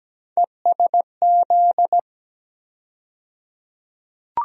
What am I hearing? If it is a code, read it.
Morse code, heard as a keyed tone, about 17 wpm, 700 Hz. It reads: ESZ
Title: USA Call Sign Postfixes - 3 Letters 17wpm